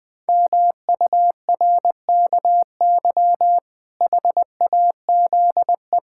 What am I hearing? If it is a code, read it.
Morse code, heard as a keyed tone, about 20 wpm, 700 Hz. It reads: MURKY HAZE
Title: Pangram Challenge v2 - 20wpm